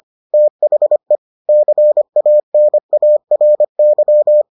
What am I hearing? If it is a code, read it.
Morse code, heard as a keyed tone, about 25 wpm, 600 Hz. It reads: THE CANARY